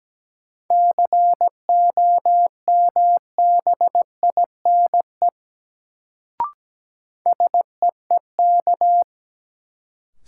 Morse code, heard as a keyed tone, about 17 wpm, 700 Hz. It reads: COMBINE SEEK
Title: Sentences from Top 1600 Words - New Word Drill - 17wpm